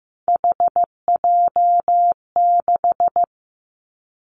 Morse code, heard as a keyed tone, about 15 wpm, 700 Hz. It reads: HJ6
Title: Callsign Prefixes for Top 100 DXCC - 15wpm